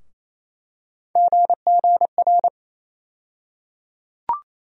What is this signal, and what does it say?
Morse code, heard as a keyed tone, about 28 wpm, 700 Hz. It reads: GGR
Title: USA Call Sign Postfixes - 3 Letters 28wpm